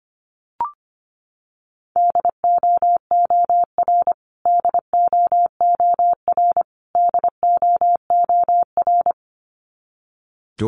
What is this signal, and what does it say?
Morse code, heard as a keyed tone, about 25 wpm, 700 Hz. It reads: DOOR DOOR DOOR
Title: Top 500 Words Repeated 25wpm